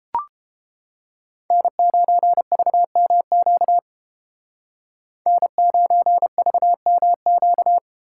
Morse code, heard as a keyed tone, about 32 wpm, 700 Hz. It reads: N9VMQ N9VMQ